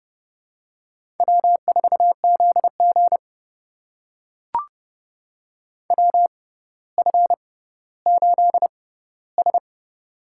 Morse code, heard as a keyed tone, about 30 wpm, spaced as for 15 wpm, 700 Hz. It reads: W4ZG WF8S
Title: USA Call Signs - 6x Character Spacing - 30wpm